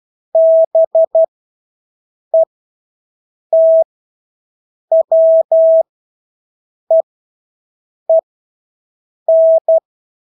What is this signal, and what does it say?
Morse code, heard as a keyed tone, about 12 wpm, spaced as for 6 wpm, 650 Hz. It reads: BETWEEN
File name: Engelske ord 12-6 wpm